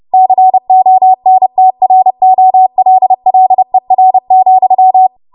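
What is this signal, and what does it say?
Morse code, heard as a keyed tone, about 30 wpm, 750 Hz. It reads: CONTROLLER,
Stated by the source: Morse practice transmission at 30 WPM.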